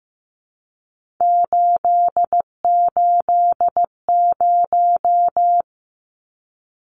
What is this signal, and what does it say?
Morse code, heard as a keyed tone, about 15 wpm, 700 Hz. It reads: 880
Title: Numbers - 3-Digit 15wpm